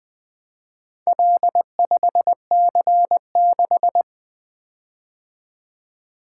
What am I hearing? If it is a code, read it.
Morse code, heard as a keyed tone, about 20 wpm, 700 Hz. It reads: L5C6
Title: Four Letter-Number Combinations - 20wpm